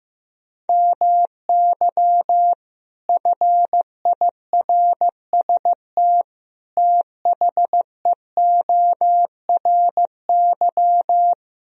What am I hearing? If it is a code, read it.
Morse code, heard as a keyed tone, about 16 wpm, 700 Hz. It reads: MY FIRST THEORY
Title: Sentences from Top 1000 Words - Review Part 3 - 15wpm